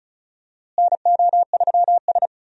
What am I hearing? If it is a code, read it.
Morse code, heard as a keyed tone, about 34 wpm, 700 Hz. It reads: NO3S
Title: USA Call Signs - 4 Characters - Repeated 3x - 35wpm